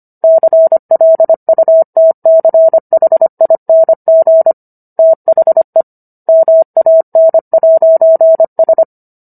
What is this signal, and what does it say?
Morse code, heard as a keyed tone, about 25 wpm, 650 Hz. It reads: CLUTCHING THE MAN'S
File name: Book One / 16-Chapter 16 - 25 WPM